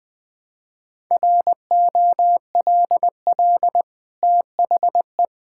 Morse code, heard as a keyed tone, about 21 wpm, 700 Hz. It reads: ROLL THE
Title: Easy Sentences with Hints - 20wpm